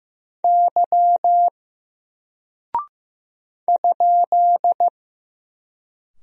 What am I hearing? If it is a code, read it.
Morse code, heard as a keyed tone, about 15 wpm, 700 Hz. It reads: Y ?
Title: Single Character (Letter:Number:Punctuation:Prosign) Alt 15wpm